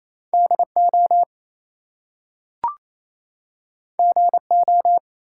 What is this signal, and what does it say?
Morse code, heard as a keyed tone, about 27 wpm, 700 Hz. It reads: DO GO
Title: Two Letter Words - 28wpm